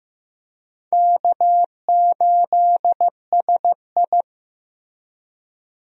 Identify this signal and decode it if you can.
Morse code, heard as a keyed tone, about 16 wpm, 700 Hz. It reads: K8SI
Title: USA Call Signs - 4 Characters - Repeated 3x - 15wpm